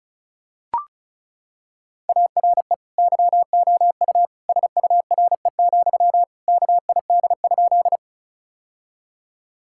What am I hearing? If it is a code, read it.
Morse code, heard as a keyed tone, about 36 wpm, 700 Hz. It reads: ARE YOU SURE, KID?
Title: Sentences from Top 1600 Words 35wpm